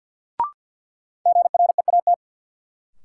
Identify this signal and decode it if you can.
Morse code, heard as a keyed tone, about 47 wpm, 700 Hz. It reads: GREAT
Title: Sentences from Top 200 Words - New Word Drill - 50wpm